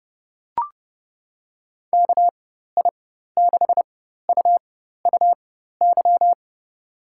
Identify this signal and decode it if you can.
Morse code, heard as a keyed tone, about 30 wpm, spaced as for 14 wpm, 700 Hz. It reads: KI6UUY
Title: USA Call Signs - 6 Characters - 4x Character Spacing - 30wpm